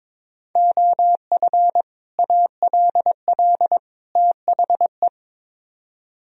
Morse code, heard as a keyed tone, about 24 wpm, 700 Hz. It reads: OF ALL THE
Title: Speed-Racing - Sets of 3 Words 22wpm